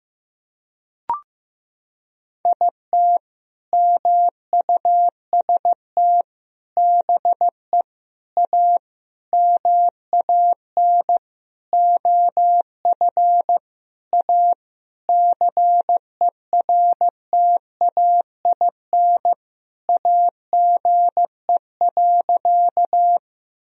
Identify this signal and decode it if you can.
Morse code, heard as a keyed tone, about 15 wpm, 700 Hz. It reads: IT MUST BE A MAN OF A CERTAIN AGE.
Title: Sentences from Top 1000 Words - Review Part 3 - 15wpm